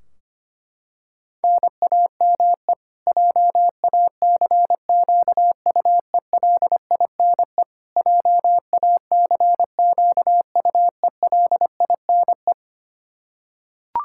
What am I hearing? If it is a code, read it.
Morse code, heard as a keyed tone, about 25 wpm, 700 Hz. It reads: NAME JACQUELINE JACQUELINE
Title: Practice QSO - Exchange 1 - Beginning - 25wpm